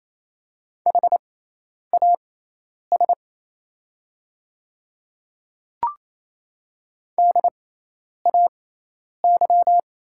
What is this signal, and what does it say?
Morse code, heard as a keyed tone, about 28 wpm, spaced as for 10 wpm, 700 Hz. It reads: HAS DAY
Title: 6x Character Spacing - Top 100 Words - 28wpm